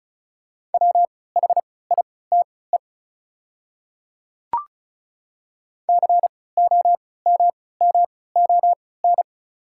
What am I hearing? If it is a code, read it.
Morse code, heard as a keyed tone, about 34 wpm, spaced as for 20 wpm, 690 Hz. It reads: WHITE COMMON
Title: 3x Character Spacing - Top 500 Words - 35wpm